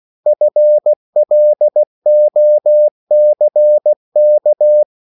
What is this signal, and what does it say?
Morse code, heard as a keyed tone, about 16 wpm, 600 Hz. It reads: FLOCK
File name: CW / Practice / EnglishSayings16